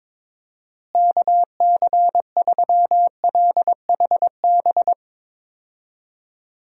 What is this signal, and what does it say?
Morse code, heard as a keyed tone, about 22 wpm, 700 Hz. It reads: KC3LHB
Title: USA Call Signs - 6 Characters - Repeated 3x - 22wpm